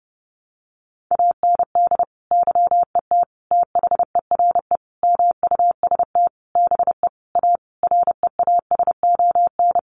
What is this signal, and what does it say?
Morse code, heard as a keyed tone, about 28 wpm, 700 Hz. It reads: AND YET THERE MUST BE A REASON
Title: Sentences from Top 1300 Words 30wpm